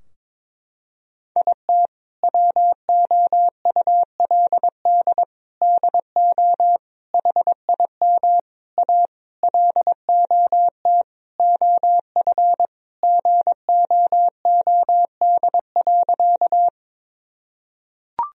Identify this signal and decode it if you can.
Morse code, heard as a keyed tone, about 22 wpm, 700 Hz. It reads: IT WOULD DO HIM A LOT OF GOOD.
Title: Sentences from Top 1000 Words - Review Part 3 - 22wpm